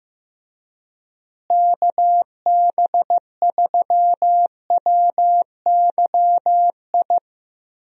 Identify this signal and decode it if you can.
Morse code, heard as a keyed tone, about 15 wpm, 700 Hz. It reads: KB3WYI
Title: USA Call Signs - 6 Characters - Repeated 3x - 15wpm